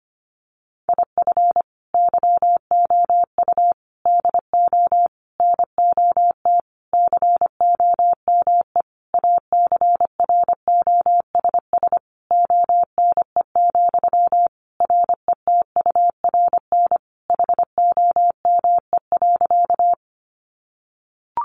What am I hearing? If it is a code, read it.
Morse code, heard as a keyed tone, about 25 wpm, 700 Hz. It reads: IF YOU DO NOT COME ACROSS ONE, RETURN HOME.
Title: Sentences from Top 800 Words 25wpm